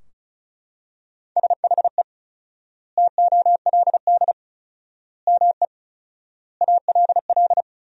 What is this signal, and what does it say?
Morse code, heard as a keyed tone, about 34 wpm, 700 Hz. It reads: SHE TOLD ME ALL
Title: 4x Word-Spacing - Sentences from Top 400 Words - 35wpm